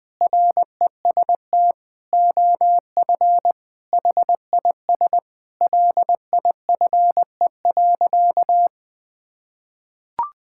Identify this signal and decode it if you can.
Morse code, heard as a keyed tone, about 20 wpm, 700 Hz. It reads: REST OF HIS LIFE.
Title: Sentences with 6-Word Phrases - 20wpm